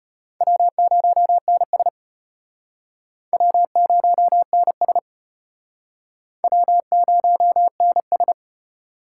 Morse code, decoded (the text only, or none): W0NS W0NS W0NS